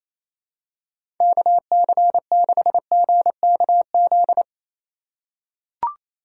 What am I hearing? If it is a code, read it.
Morse code, heard as a keyed tone, about 28 wpm, 700 Hz. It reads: KC6GKZ
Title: USA Call Signs - 2x Character Spacing - 28wpm